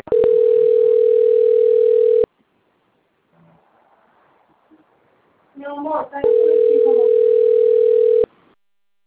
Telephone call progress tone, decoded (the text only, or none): ringback tone